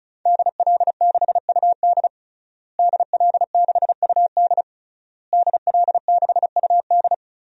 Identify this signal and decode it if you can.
Morse code, heard as a keyed tone, about 36 wpm, 700 Hz. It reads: DL6UD DL6UD DL6UD
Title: DXCC-100 Call Signs - 5 Characters - Repeated 3x - 35wpm